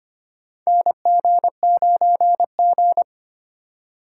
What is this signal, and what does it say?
Morse code, heard as a keyed tone, about 25 wpm, 700 Hz. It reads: NG9G
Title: USA Call Signs - 4 Characters - 2x Character Spacing - 25wpm